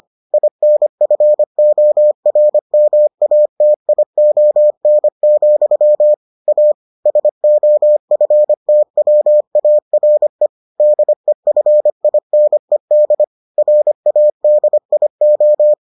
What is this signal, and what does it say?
Morse code, heard as a keyed tone, about 25 wpm, 600 Hz. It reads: INFORMATION, A SOFTWARE DEFINED RADIO